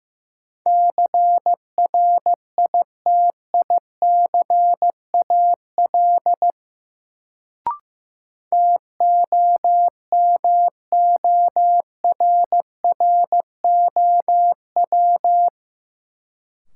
Morse code, heard as a keyed tone, about 15 wpm, 700 Hz. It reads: CRITICAL TOMORROW
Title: Sentences from Top 1100 Words - New Word Drill - 15wpm